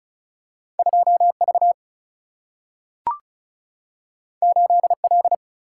Morse code, heard as a keyed tone, about 36 wpm, 700 Hz. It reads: 2V 8L
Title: Single-Letter Single-Number 35wpm